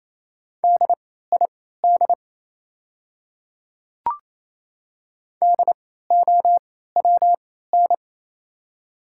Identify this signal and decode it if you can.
Morse code, heard as a keyed tone, about 28 wpm, spaced as for 16 wpm, 700 Hz. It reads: DID DOWN
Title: 3x Character Spacing - Top 300 Words - 28wpm